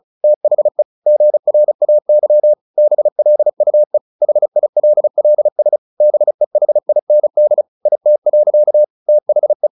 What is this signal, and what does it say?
Morse code, heard as a keyed tone, about 34 wpm, 600 Hz. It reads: THE GRAY BLUE HILLS BEHIND IT. THE
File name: the old man and the sea / the old man and the sea 1 35wpm